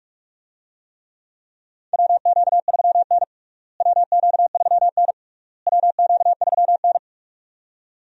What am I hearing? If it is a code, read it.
Morse code, heard as a keyed tone, about 46 wpm, 700 Hz. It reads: WQ3N WQ3N WQ3N
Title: USA Call Signs - 4 Characters - Repeated 3x - 45wpm